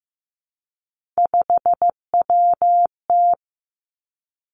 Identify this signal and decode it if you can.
Morse code, heard as a keyed tone, about 15 wpm, 700 Hz. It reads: 5WT
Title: Three Challenging Letter-Number Combinations 15wpm